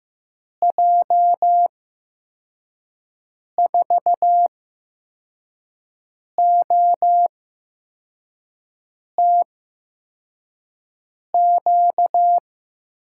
Morse code, heard as a keyed tone, about 15 wpm, spaced as for 4 wpm, 700 Hz. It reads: J4OTQ
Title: USA Call Signs - 6 Characters - 8x Character Spacing - 15wpm